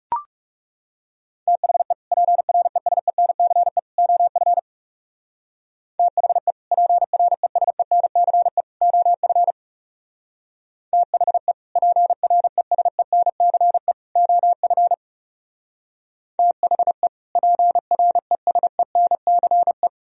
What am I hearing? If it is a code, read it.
Morse code, heard as a keyed tone, about 37 wpm, 700 Hz. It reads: THE PRESENCE OF THE PRESENCE OF THE PRESENCE OF THE PRESENCE
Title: Speed-Racing - Sets of 3 Words 30wpm